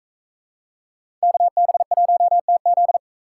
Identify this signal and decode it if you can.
Morse code, heard as a keyed tone, about 44 wpm, 700 Hz. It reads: KB1TZ